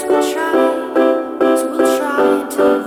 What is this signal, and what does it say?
There is no code.